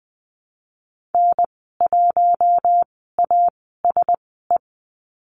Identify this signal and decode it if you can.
Morse code, heard as a keyed tone, about 20 wpm, spaced as for 15 wpm, 700 Hz. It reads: N1ASE
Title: 048 Lesson 9g - Number 4 - Call Signs 20wpm